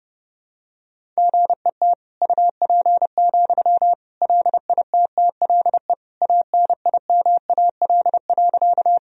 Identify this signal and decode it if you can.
Morse code, heard as a keyed tone, about 30 wpm, 700 Hz. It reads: GET UP, LITTLE ANIMAL.
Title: Sentences from Top 1000 Words - Review Part 1 - 30wpm